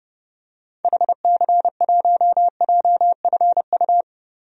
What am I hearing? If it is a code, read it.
Morse code, heard as a keyed tone, about 30 wpm, 700 Hz. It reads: HC1JFU
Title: DXCC-100 Call Signs - 6 Characters - Repeated 3x - 30wpm